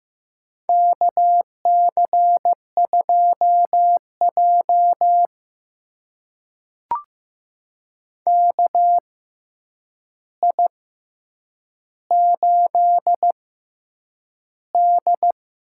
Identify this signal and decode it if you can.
Morse code, heard as a keyed tone, about 15 wpm, spaced as for 8 wpm, 700 Hz. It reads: KC2J KI8D
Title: USA Call Signs - 4 Characters - 6x Character Spacing - 15wpm